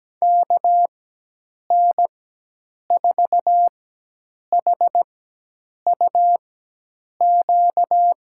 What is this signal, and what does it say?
Morse code, heard as a keyed tone, about 17 wpm, spaced as for 8 wpm, 700 Hz. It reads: KN4HUQ
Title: USA Call Signs - 4x Character Spacing - 17wpm